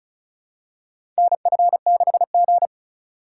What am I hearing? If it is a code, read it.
Morse code, heard as a keyed tone, about 36 wpm, 700 Hz. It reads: NF6G
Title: USA Call Signs - 4 Characters - 4x Character Spacing - 35wpm